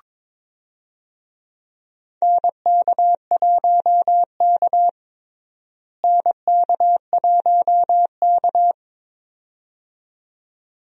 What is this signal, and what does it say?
Morse code, heard as a keyed tone, about 21 wpm, 700 Hz. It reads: NK1K NK1K
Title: USA Call Signs - 4 Characters - Repeated 2x - 22wpm